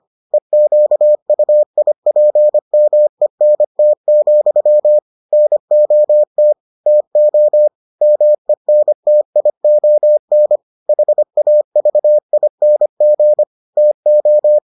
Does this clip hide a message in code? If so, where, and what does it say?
Yes: Morse code from 0.3 to 14.6 s, heard as a keyed tone, about 25 wpm, 600 Hz. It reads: EQUIPMENT, NOT TO MENTION HAVING TO